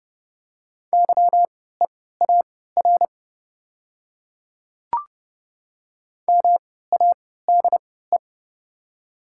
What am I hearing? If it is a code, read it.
Morse code, heard as a keyed tone, about 30 wpm, spaced as for 17 wpm, 700 Hz. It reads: YEAR MADE
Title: 3x Character Spacing - Top 200 Words - 30wpm